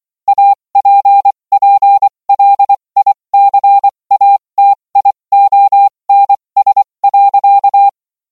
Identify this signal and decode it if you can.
Morse code, heard as a keyed tone, about 24 wpm, spaced as for 20 wpm, 800 Hz. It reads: APPLICATIONS.